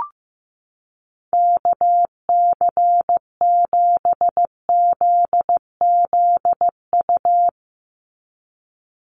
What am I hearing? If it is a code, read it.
Morse code, heard as a keyed tone, about 15 wpm, 700 Hz. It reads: KC7ZZU
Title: USA Call Signs 15wpm